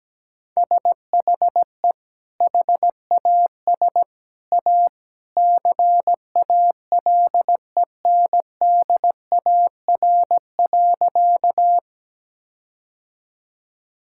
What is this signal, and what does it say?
Morse code, heard as a keyed tone, about 17 wpm, 700 Hz. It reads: SHE HAS A CALENDAR.
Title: Sentences from Top 1600 Words 17wpm